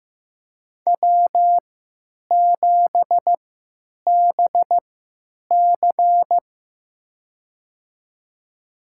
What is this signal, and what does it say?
Morse code, heard as a keyed tone, about 15 wpm, spaced as for 9 wpm, 700 Hz. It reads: W7BC
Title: USA Call Signs - 4 Characters - 3x Character Spacing - 15wpm